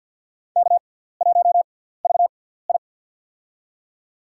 Morse code, heard as a keyed tone, about 53 wpm, spaced as for 17 wpm, 700 Hz. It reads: K1VI
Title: USA Call Signs - 4 Characters - 6x Character Spacing - 50wpm